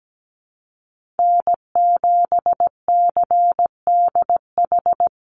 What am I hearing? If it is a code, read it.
Morse code, heard as a keyed tone, about 17 wpm, 700 Hz. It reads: N7CDH
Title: USA Call Signs Repeated 3x 17wpm